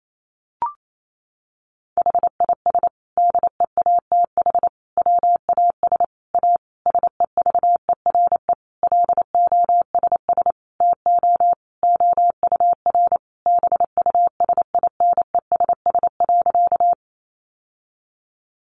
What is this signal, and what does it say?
Morse code, heard as a keyed tone, about 28 wpm, 700 Hz. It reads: HIS DEATH WAS A SEVERE LOSS TO OUR BUSINESS.
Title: Sentences from Top 1400 Words 28wpm